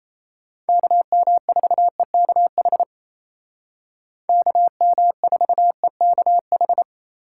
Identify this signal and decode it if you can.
Morse code, heard as a keyed tone, about 36 wpm, 700 Hz. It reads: KM4EKH KM4EKH